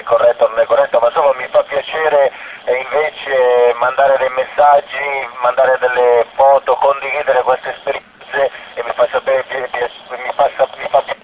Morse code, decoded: none